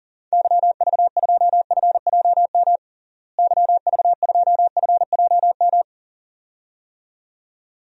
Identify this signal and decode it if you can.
Morse code, heard as a keyed tone, about 40 wpm, 700 Hz. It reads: YV2FJM YV2FJM
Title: DXCC-100 Call Signs - 6 Characters - Repeated 2x - 40wpm